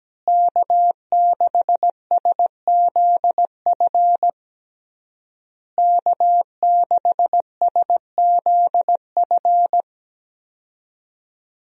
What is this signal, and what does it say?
Morse code, heard as a keyed tone, about 17 wpm, 700 Hz. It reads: K6SZF K6SZF
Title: USA Call Signs - 5 Characters - Repeated 3x - 17wpm